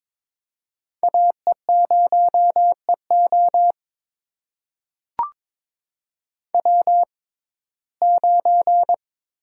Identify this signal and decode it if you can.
Morse code, heard as a keyed tone, about 21 wpm, spaced as for 16 wpm, 700 Hz. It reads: AE0EO W9
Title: USA Call Signs - 5 Characters - 6x Character Spacing - 22wpm